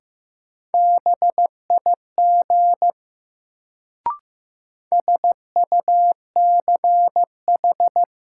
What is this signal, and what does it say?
Morse code, heard as a keyed tone, about 15 wpm, 700 Hz. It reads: BIG SUCH
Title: Sets of 2 Words - New Word Drill - 15wpm